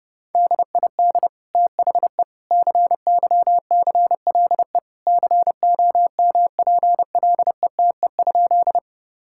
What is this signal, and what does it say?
Morse code, heard as a keyed tone, about 29 wpm, 700 Hz. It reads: DID THE CYCLE COMPLETE?
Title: Sentences from Top 1000 Words - Review Part 6 - 30wpm